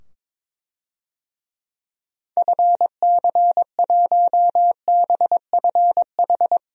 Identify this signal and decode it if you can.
Morse code, heard as a keyed tone, about 22 wpm, 700 Hz. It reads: FC1BFH
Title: DXCC-100 Call Signs - 6 Characters - 22wpm